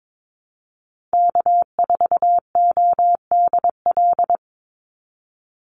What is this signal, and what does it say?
Morse code, heard as a keyed tone, about 22 wpm, 700 Hz. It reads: K4ODL
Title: Call Signs Repeated 3x - 22wpm